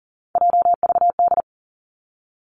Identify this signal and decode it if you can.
Morse code, heard as a keyed tone, about 38 wpm, 700 Hz. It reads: JVD
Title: Three Challenging Letter-Number Combinations 40wpm